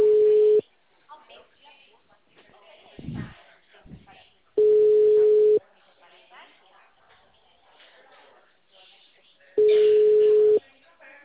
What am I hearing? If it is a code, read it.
Ringback tone (Europe, 425 Hz), with speech.
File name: out- -2007- - - .32